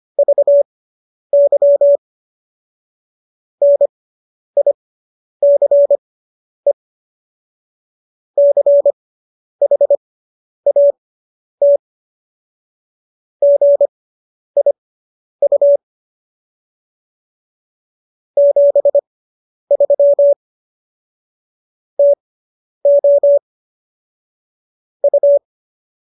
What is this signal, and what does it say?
Morse code, heard as a keyed tone, about 25 wpm, spaced as for 10 wpm, 570 Hz. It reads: VY NICE CHAT GIU 73 TO U